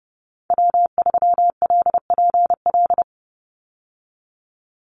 Morse code, heard as a keyed tone, about 30 wpm, 700 Hz. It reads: W3LPL
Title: Call Signs Repeated 3x - 30wpm